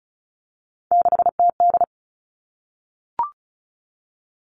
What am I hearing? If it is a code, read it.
Morse code, heard as a keyed tone, about 34 wpm, 700 Hz. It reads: 6TD
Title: All Cut Numbers - 3-Digit 35wpm